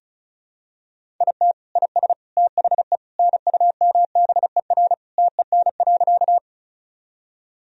Morse code, heard as a keyed tone, about 34 wpm, 700 Hz. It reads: IT IS THE NUMBER TEN.
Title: Sentences from Top 1000 Words - Review Part 2 - 35wpm